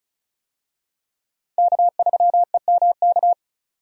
Morse code, heard as a keyed tone, about 36 wpm, 700 Hz. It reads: K3EMK
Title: USA Call Signs - 5 Characters - 35wpm